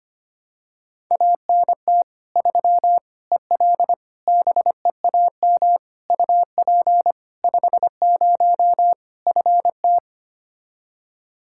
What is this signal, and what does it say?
Morse code, heard as a keyed tone, about 25 wpm, 700 Hz. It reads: ANT 3 EL BEAM UP 50 FT
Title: Practice QSO - Exchange 2 - Info - 25wpm